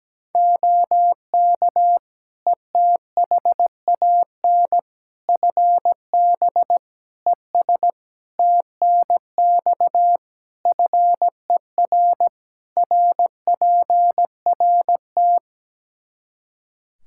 Morse code, heard as a keyed tone, about 17 wpm, 700 Hz. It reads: OK ETHAN FB ES TNX FER RPRT
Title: Practice QSO - Exchange 2 - Info - 17wpm